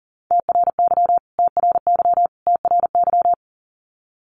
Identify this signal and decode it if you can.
Morse code, heard as a keyed tone, about 40 wpm, 700 Hz. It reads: TRY TRY TRY
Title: Top 300 Words Repeated 40wpm